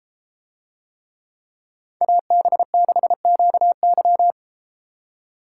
Morse code, heard as a keyed tone, about 32 wpm, 700 Hz. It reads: AB6QY